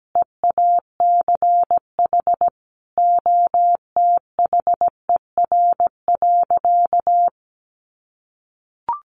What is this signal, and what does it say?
Morse code, heard as a keyed tone, about 17 wpm, 700 Hz. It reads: EACH OTHER.
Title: Sentences from Top 700 Words 17wpm